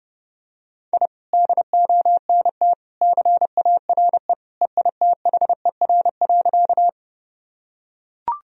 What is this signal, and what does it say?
Morse code, heard as a keyed tone, about 30 wpm, 700 Hz. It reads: I DONT CARE EITHER.
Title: Sentences from Top 1000 Words - Review Part 4 - 30wpm